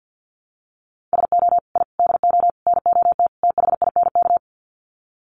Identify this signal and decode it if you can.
Morse code, heard as a keyed tone, about 50 wpm, 700 Hz. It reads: SO I DO NOT THINK
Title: Sets of 5 Words 50wpm